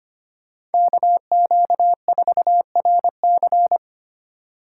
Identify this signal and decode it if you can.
Morse code, heard as a keyed tone, about 25 wpm, 700 Hz. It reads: KQ4RC
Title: USA Call Signs - 5 Characters - 2x Character Spacing - 25wpm